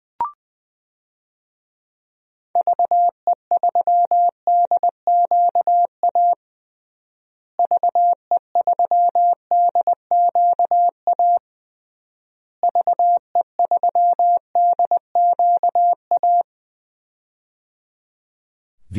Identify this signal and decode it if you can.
Morse code, heard as a keyed tone, about 20 wpm, 700 Hz. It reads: VE3DQA VE3DQA VE3DQA
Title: DXCC-100 Call Signs - 6 Characters - Repeated 3x - 20wpm